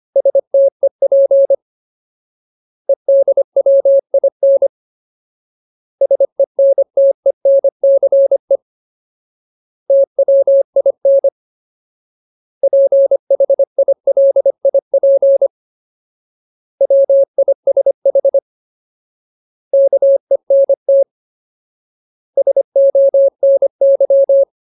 Morse code, heard as a keyed tone, about 25 wpm, 550 Hz. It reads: STEP EDWIN SENTENCE TWIN PHILIP WISH KENT SONY
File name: Ultimate Word List / 25WPM Ultimate